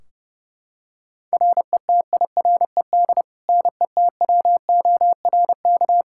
Morse code, heard as a keyed tone, about 30 wpm, 700 Hz. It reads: RETIRED NETWORK
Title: Practice QSO - Exchange 3 - More Info - 30wpm